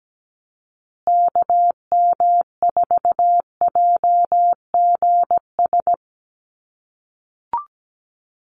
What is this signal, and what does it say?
Morse code, heard as a keyed tone, about 17 wpm, 700 Hz. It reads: KM4JGS
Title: USA Call Signs Repeated 3x 17wpm